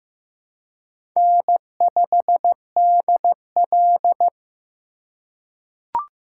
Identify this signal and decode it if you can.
Morse code, heard as a keyed tone, about 15 wpm, 700 Hz. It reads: N5DL
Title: USA Call Signs - 4 Characters - 2x Character Spacing - 15wpm